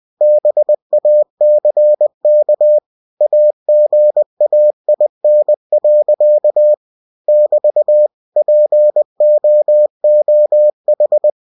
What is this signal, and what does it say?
Morse code, heard as a keyed tone, about 20 wpm, 600 Hz. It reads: BACK AGAIN. <BT> POOH